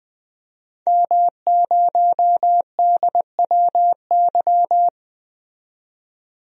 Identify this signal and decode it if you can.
Morse code, heard as a keyed tone, about 20 wpm, 700 Hz. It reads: M0DWY
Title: DXCC-100 Call Signs - 5 Characters - 20wpm